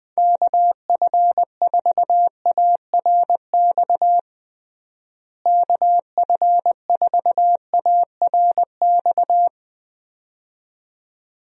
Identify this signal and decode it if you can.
Morse code, heard as a keyed tone, about 21 wpm, 700 Hz. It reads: KF4ARX KF4ARX
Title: USA Call Signs - 6 Characters - Repeated 2x - 20wpm